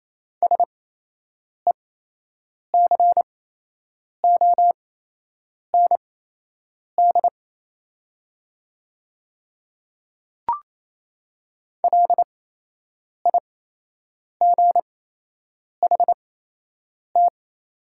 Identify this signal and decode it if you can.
Morse code, heard as a keyed tone, about 28 wpm, spaced as for 8 wpm, 700 Hz. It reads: SECOND LIGHT
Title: 8x Character Spacing - Top 500 Words - 28wpm